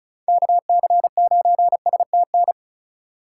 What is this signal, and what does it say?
Morse code, heard as a keyed tone, about 36 wpm, 700 Hz. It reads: KC9STN
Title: USA Call Signs - 6 Characters - 2x Character Spacing - 35wpm